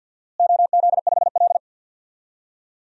Morse code, heard as a keyed tone, about 48 wpm, 700 Hz. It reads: OZ5L
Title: DXCC-100 Call Signs - 4 Characters - Repeated 2x - 50wpm